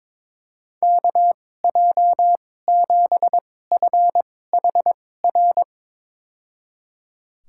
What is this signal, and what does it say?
Morse code, heard as a keyed tone, about 22 wpm, spaced as for 16 wpm, 700 Hz. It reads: KJ7FHR
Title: USA Call Signs - 6 Characters - 2x Character Spacing - 22wpm